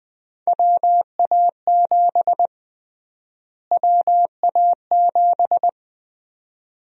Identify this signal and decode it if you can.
Morse code, heard as a keyed tone, about 20 wpm, 700 Hz. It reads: WA7 WA7
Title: USA Call Sign Prefixes - Repeated 2x - 20wpm